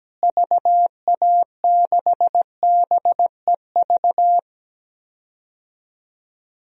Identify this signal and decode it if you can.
Morse code, heard as a keyed tone, about 17 wpm, 700 Hz. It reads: VA6BEV
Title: DXCC-100 Call Signs - 6 Characters - Repeated 3x - 17wpm